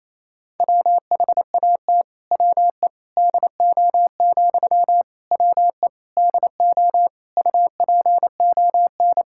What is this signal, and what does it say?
Morse code, heard as a keyed tone, about 28 wpm, 700 Hz. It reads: WHAT WE DO, WE DO UPON
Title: Sentences from Top 1600 Words 28wpm